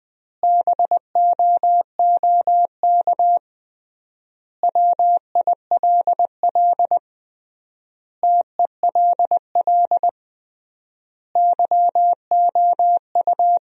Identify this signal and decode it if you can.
Morse code, heard as a keyed tone, about 20 wpm, 700 Hz. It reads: BOOK WILL TELL YOU
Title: 3x Word-Spacing - Sentences from Top 300 Words - 20wpm